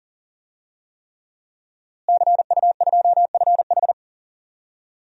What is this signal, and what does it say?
Morse code, heard as a keyed tone, about 40 wpm, 700 Hz. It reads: CU2FH
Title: DXCC-100 Call Signs - 5 Characters - Repeated 3x - 40wpm